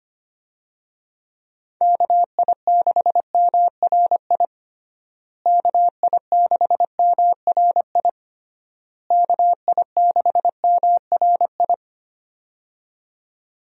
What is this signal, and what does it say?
Morse code, heard as a keyed tone, about 25 wpm, 700 Hz. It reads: KI6MRI KI6MRI KI6MRI
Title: USA Call Signs - 6 Characters - Repeated 3x - 25wpm